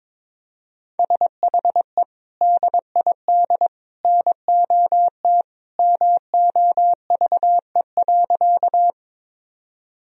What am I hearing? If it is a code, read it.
Morse code, heard as a keyed tone, about 22 wpm, 700 Hz. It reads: SHE DID NOT MOVE.
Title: Sentences from Top 1000 Words - Review Part 1 - 22wpm